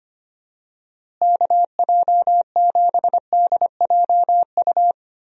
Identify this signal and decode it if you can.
Morse code, heard as a keyed tone, about 25 wpm, 700 Hz. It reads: KJ7DJU
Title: USA Call Signs - 6 Characters - Repeated 3x - 25wpm